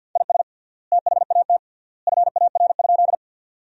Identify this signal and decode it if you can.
Morse code, heard as a keyed tone, about 52 wpm, 700 Hz. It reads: IS THAT FAR?
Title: 3x Word-Spacing - Sentences from Top 300 Words - 50wpm